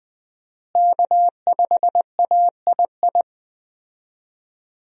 Morse code, heard as a keyed tone, about 20 wpm, 700 Hz. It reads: K5AII
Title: USA Call Signs - 5 Characters - Repeated 2x - 20wpm